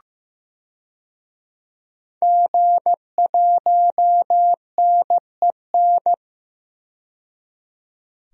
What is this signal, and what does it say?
Morse code, heard as a keyed tone, about 15 wpm, 700 Hz. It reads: G1NEN
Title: DXCC-100 Call Signs - 5 Characters - 15wpm